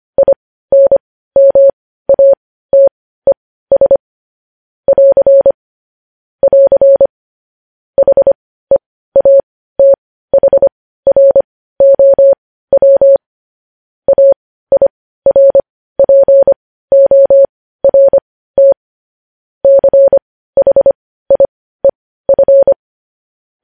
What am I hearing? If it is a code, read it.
Morse code, heard as a keyed tone, about 25 wpm, spaced as for 15 wpm, 550 Hz. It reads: INMATES <AR> <AR> HEATHROW AIRPORT CHIEF
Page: news headlines in morse code at 15 wpm